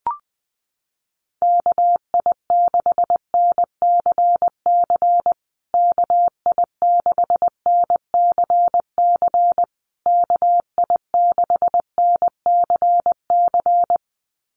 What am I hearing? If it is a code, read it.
Morse code, heard as a keyed tone, about 20 wpm, 700 Hz. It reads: KI6NCC KI6NCC KI6NCC
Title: USA Call Signs Repeated 3x 20wpm